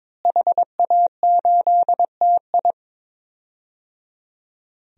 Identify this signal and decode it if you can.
Morse code, heard as a keyed tone, about 22 wpm, 700 Hz. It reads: HA8TI
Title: DXCC-100 Call Signs - 5 Characters - Repeated 3x - 22wpm